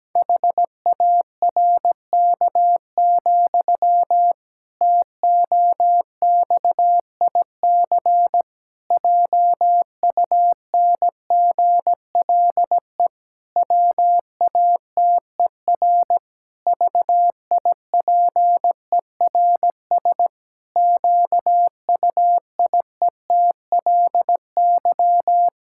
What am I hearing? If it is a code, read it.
Morse code, heard as a keyed tone, about 17 wpm, 700 Hz. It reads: HARK, TOXIC JUNGLE WATER VIPERS QUIETLY
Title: Pangram Challenge v2 - 17wpm